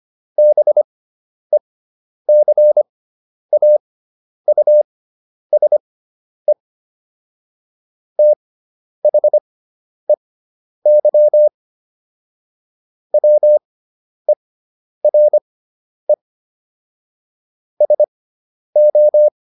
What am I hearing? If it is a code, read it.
Morse code, heard as a keyed tone, about 25 wpm, spaced as for 10 wpm, 600 Hz. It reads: BECAUSE THEY WERE SO